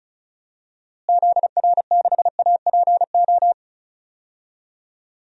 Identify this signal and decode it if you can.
Morse code, heard as a keyed tone, about 34 wpm, 700 Hz. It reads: ZR6APO
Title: DXCC-100 Call Signs - 6 Characters - Repeated 3x - 35wpm